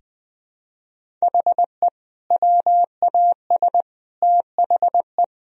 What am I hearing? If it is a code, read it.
Morse code, heard as a keyed tone, about 20 wpm, 700 Hz. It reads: HE WAS THE
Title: Sentences from Top 1000 Words - Review Part 1 - 20wpm